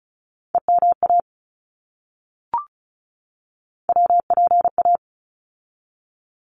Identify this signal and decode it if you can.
Morse code, heard as a keyed tone, about 34 wpm, 700 Hz. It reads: EMA WPA
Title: ARRL Sections 35wpm